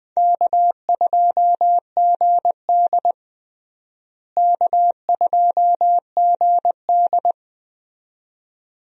Morse code, heard as a keyed tone, about 20 wpm, 700 Hz. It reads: K2GD K2GD
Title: USA Call Signs - 4 Characters - Repeated 2x - 20wpm